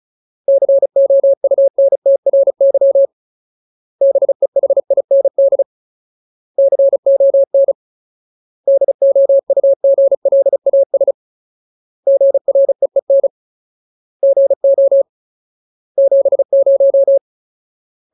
Morse code, heard as a keyed tone, about 34 wpm, 550 Hz. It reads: COUNTRY BEHIND CON DOUGLAS GREEN GO 70
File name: Ultimate Word List / 35WPM Ultimate